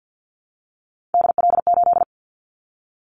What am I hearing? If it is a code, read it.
Morse code, heard as a keyed tone, about 50 wpm, 700 Hz. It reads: DL8
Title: Callsign Prefixes for Top 50 DXCC - 50wpm